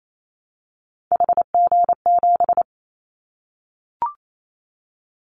Morse code, heard as a keyed tone, about 28 wpm, 700 Hz. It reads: HG7
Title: Callsign Prefixes for Top 50 DXCC - 28wpm